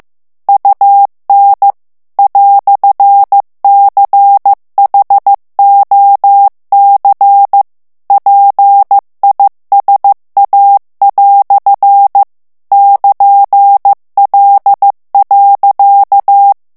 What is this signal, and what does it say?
Morse code, heard as a keyed tone, about 15 wpm, 800 Hz. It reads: UN "CHOC PISA" <KN>L.
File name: morse mp3 / FRANCAIS TEXTE CLAIR 1 VITESSE 14 MOTS MINUTES